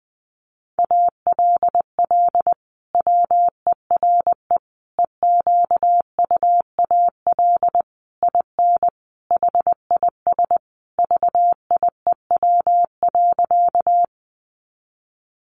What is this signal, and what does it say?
Morse code, heard as a keyed tone, about 20 wpm, 700 Hz. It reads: ALL WERE EQUAL IN HIS VIEW.
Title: Sentences from Top 1400 Words 20wpm